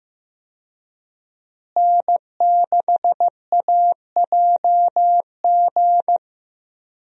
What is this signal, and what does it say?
Morse code, heard as a keyed tone, about 15 wpm, 700 Hz. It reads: N6AJG
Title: USA Call Signs - 5 Characters - Repeated 2x - 15wpm